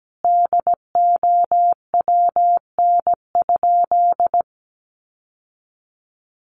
Morse code, heard as a keyed tone, about 17 wpm, 700 Hz. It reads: DOWN?
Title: Sentences from Top 1300 Words 17wpm